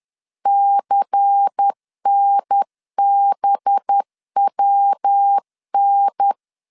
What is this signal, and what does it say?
Morse code, heard as a keyed tone, about 10 wpm, 800 Hz. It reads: CNBWN
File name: CW / DET / 1 6 1500C S C 10wpm 4252